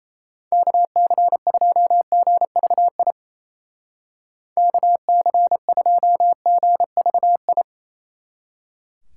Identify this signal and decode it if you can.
Morse code, heard as a keyed tone, about 37 wpm, 700 Hz. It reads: KC2GVI KC2GVI